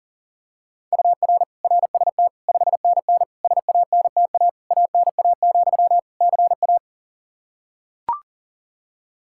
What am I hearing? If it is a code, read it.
Morse code, heard as a keyed tone, about 40 wpm, 700 Hz. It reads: UR RST 5NN SANTA ANA, CA
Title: Practice QSO - Exchange 1 - Beginning - 40wpm